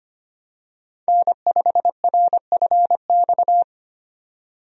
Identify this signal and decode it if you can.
Morse code, heard as a keyed tone, about 25 wpm, 700 Hz. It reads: N5RFX
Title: USA Call Signs - 5 Characters - 25wpm